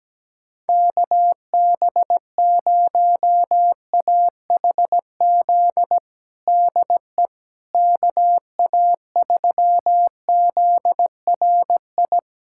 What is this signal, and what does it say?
Morse code, heard as a keyed tone, about 17 wpm, 700 Hz. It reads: KB0AHZ DE KA3ZRI
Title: USA Callsign DE Callsign - 6 Characters - 17wpm